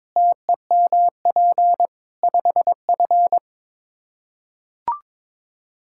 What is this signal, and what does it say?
Morse code, heard as a keyed tone, about 21 wpm, 700 Hz. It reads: TEMP 5F
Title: Practice QSO - Exchange 2 - Info - 22wpm